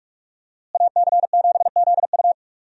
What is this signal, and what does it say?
Morse code, heard as a keyed tone, about 42 wpm, 700 Hz. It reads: AC7ZU
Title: USA Call Signs - 5 Characters - Repeated 3x - 45wpm